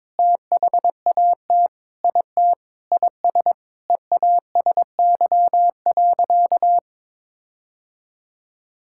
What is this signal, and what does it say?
Morse code, heard as a keyed tone, about 23 wpm, 700 Hz. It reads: THAT IT IS EASY.
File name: Sentences with 7-Word Phrases - 22wpm